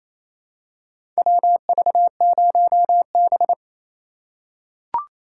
Morse code, heard as a keyed tone, about 28 wpm, 700 Hz. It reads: WV0B
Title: USA Call Signs - 4 Characters - 3x Character Spacing - 28wpm